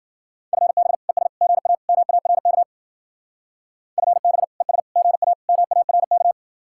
Morse code, heard as a keyed tone, about 50 wpm, 700 Hz. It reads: FB ES GA MARK FB ES GA MARK
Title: Speed-Racing Common QSO Phrases Encore - 40wpm